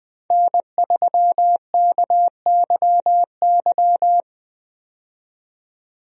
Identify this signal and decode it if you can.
Morse code, heard as a keyed tone, about 20 wpm, 700 Hz. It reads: N3KYY
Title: USA Call Signs - 5 Characters - 20wpm